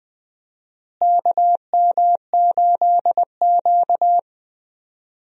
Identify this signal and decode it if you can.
Morse code, heard as a keyed tone, about 20 wpm, 700 Hz. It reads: KM8Q
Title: USA Call Signs - 4 Characters - Repeated 2x - 20wpm